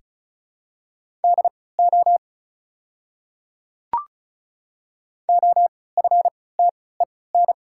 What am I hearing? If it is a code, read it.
Morse code, heard as a keyed tone, about 34 wpm, spaced as for 20 wpm, 700 Hz. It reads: DO OFTEN
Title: 3x Character Spacing - Top 300 Words - 35wpm